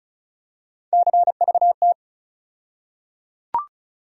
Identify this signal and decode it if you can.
Morse code, heard as a keyed tone, about 34 wpm, 700 Hz. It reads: CVT
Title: USA Call Sign Postfixes - 3 Letters 35wpm